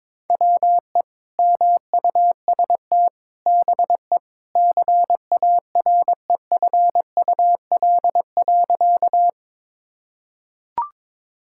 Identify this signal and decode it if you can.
Morse code, heard as a keyed tone, about 21 wpm, 700 Hz. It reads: WE MUST BE CAREFUL.
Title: Sentences from Top 1600 Words 22wpm